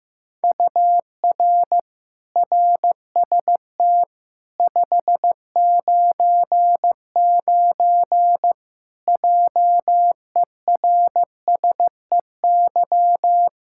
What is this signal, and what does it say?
Morse code, heard as a keyed tone, about 15 wpm, 700 Hz. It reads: UR RST 599 JERSEY
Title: Practice QSO - Exchange 1 - Beginning - 15wpm